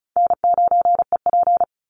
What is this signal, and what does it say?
Morse code, heard as a keyed tone, about 34 wpm, 700 Hz. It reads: N9EP
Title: Easy Call Signs Repeated 2x - 35wpm